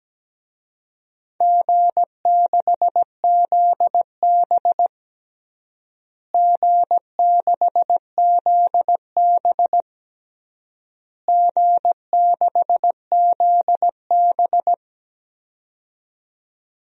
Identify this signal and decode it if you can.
Morse code, heard as a keyed tone, about 17 wpm, 700 Hz. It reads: G6ZB G6ZB G6ZB
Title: DXCC-100 Call Signs - 4 Characters - Repeated 3x - 17wpm